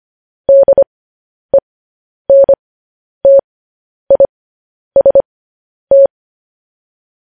Morse code, heard as a keyed tone, about 25 wpm, spaced as for 10 wpm, 550 Hz. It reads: DENTIST
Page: news-headlines-in-morse-code-at-10-wpm